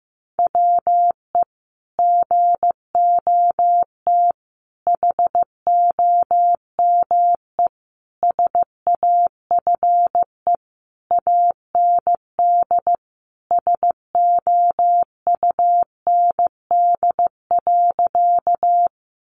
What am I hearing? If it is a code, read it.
Morse code, heard as a keyed tone, about 15 wpm, 700 Hz. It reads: WE GOT HOME SAFE AND SOUND.
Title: Sentences from Top 900 Words 15wpm